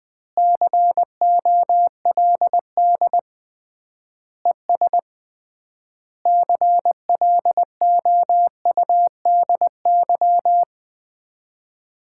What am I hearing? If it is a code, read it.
Morse code, heard as a keyed tone, about 20 wpm, 700 Hz. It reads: COLD ES CLOUDY
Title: Common QSO Phrases - 3x Word-Spacing - 20wpm